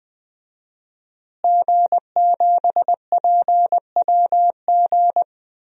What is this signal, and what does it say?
Morse code, heard as a keyed tone, about 21 wpm, 700 Hz. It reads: G7PWG
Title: DXCC-100 Call Signs - 5 Characters - Repeated 3x - 20wpm